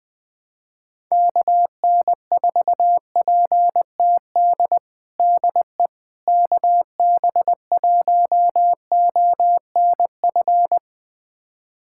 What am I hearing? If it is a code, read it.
Morse code, heard as a keyed tone, about 20 wpm, 700 Hz. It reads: KN4PTD DE KB1ONF
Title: USA Callsign DE Callsign - 6 Characters - 20wpm